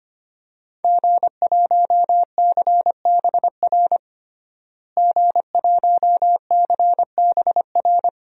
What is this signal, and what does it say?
Morse code, heard as a keyed tone, about 25 wpm, 700 Hz. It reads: G1CBR G1CBR
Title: DXCC-100 Call Signs - 5 Characters - Repeated 3x - 25wpm